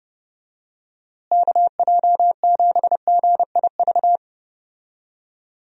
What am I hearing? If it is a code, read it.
Morse code, heard as a keyed tone, about 32 wpm, 700 Hz. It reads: KJ7GIV
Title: USA Call Signs - 6 Characters - Repeated 2x - 30wpm